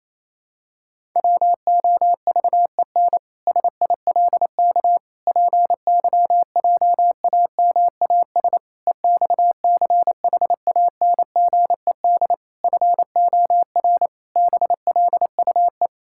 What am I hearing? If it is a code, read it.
Morse code, heard as a keyed tone, about 28 wpm, 700 Hz. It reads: WOVEN SILK PYJAMAS EXCHANGED FOR BLUE
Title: Pangram Challenge v2 - 28wpm